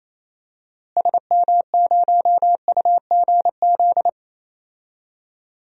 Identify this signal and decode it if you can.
Morse code, heard as a keyed tone, about 28 wpm, 700 Hz. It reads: SM0UGZ
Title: DXCC-100 Call Signs - 6 Characters - Repeated 3x - 28wpm